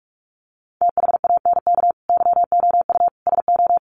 Morse code, heard as a keyed tone, about 44 wpm, 700 Hz. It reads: THANK YOU SO
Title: Sets of 4 Words 45wpm